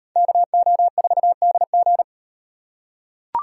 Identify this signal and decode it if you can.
Morse code, heard as a keyed tone, about 36 wpm, 700 Hz. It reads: KO4DG